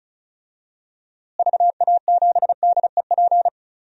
Morse code, heard as a keyed tone, about 34 wpm, 700 Hz. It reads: VA7DEP
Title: DXCC-100 Call Signs - 6 Characters - Repeated 3x - 35wpm